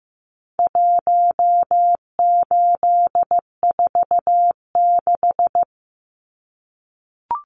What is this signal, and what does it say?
Morse code, heard as a keyed tone, about 15 wpm, 700 Hz. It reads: 1846
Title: Numbers - Contest Style - 4-Digit 15wpm